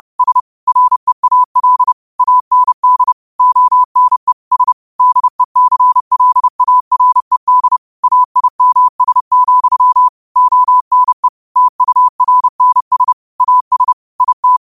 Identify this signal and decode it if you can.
Morse code, heard as a keyed tone, about 30 wpm, 1000 Hz. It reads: S REAL AND ONE S DECLARED AIMS, ONE TURNS AS IT
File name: qotdfiles / 30wpm